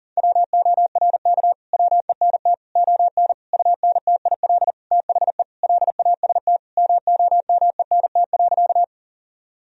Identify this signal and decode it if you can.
Morse code, heard as a keyed tone, about 40 wpm, 700 Hz. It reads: WORK WENT ON UNTIL THE LAST MOMENT.
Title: Sentences from Top 1000 Words - Review Part 1 - 40wpm